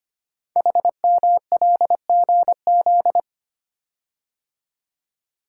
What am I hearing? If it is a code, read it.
Morse code, heard as a keyed tone, about 25 wpm, 700 Hz. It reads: HMLGZ
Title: Code Groups - 5 Letters - 25wpm